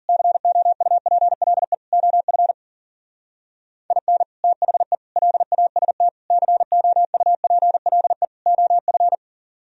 Morse code, heard as a keyed tone, about 51 wpm, 700 Hz. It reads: COUPLE OF IN THE LAST COUPLE OF